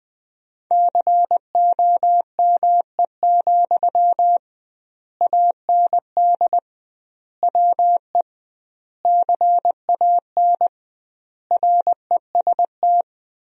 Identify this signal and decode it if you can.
Morse code, heard as a keyed tone, about 21 wpm, 700 Hz. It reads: COME, AND WE CAN REST
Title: 2x Word-Spacing - Sentences from Top 500 Words - 20wpm